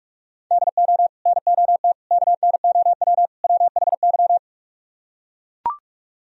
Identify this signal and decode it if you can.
Morse code, heard as a keyed tone, about 44 wpm, 700 Hz. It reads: DO NOT KNOW WHY